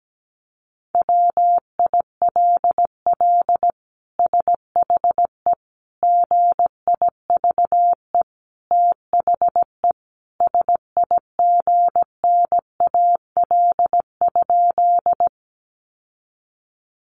Morse code, heard as a keyed tone, about 17 wpm, 700 Hz. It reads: WILL SHE GIVE THE SIGNAL?
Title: Sentences from Top 1400 Words 17wpm